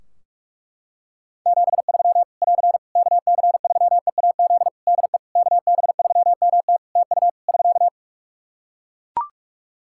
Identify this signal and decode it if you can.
Morse code, heard as a keyed tone, about 41 wpm, 700 Hz. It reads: 73 <AR> KC3EAZ DE KB3MT TU <SK>
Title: Practice QSO - Exchange 4 - Ending - 45wpm